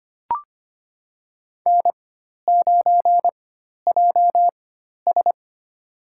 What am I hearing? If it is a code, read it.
Morse code, heard as a keyed tone, about 25 wpm, spaced as for 12 wpm, 700 Hz. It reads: N9JS
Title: USA Call Signs - 4 Characters - 4x Character Spacing - 25wpm